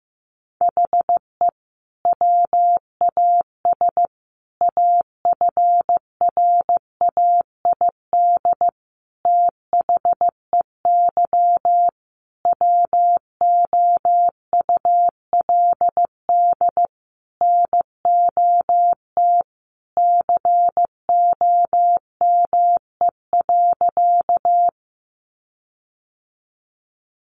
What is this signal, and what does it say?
Morse code, heard as a keyed tone, about 15 wpm, 700 Hz. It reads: HE WAS AFRAID THEY WOULD NOT COME.
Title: Sentences from Top 1500 Words 15wpm